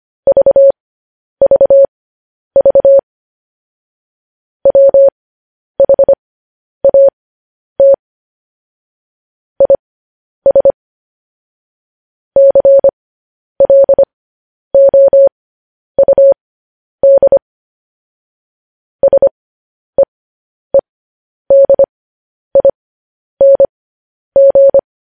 Morse code, heard as a keyed tone, about 25 wpm, spaced as for 10 wpm, 550 Hz. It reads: VVV WHAT IS CLOUD SEEDING
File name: podcast / News headlines-1x-25x10